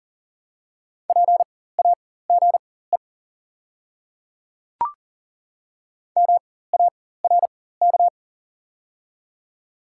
Morse code, heard as a keyed tone, about 40 wpm, spaced as for 19 wpm, 690 Hz. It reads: PAGE MARK
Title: 4x Character Spacing - Top 300 Words - 40wpm